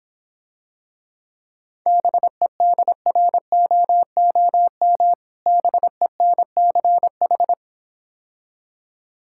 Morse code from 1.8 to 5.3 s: BEDROOM

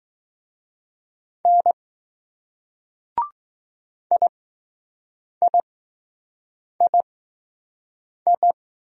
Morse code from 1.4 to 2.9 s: N

